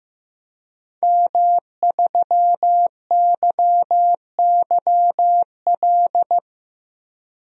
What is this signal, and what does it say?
Morse code, heard as a keyed tone, about 15 wpm, 700 Hz. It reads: M3YYL
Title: DXCC-100 Call Signs - 5 Characters - Repeated 2x - 15wpm